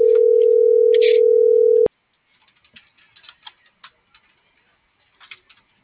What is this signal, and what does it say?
Ringback tone (North America), 440 + 480 Hz.